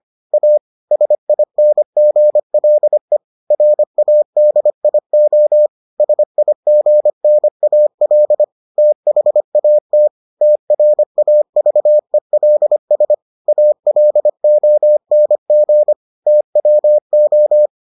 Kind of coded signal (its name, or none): Morse code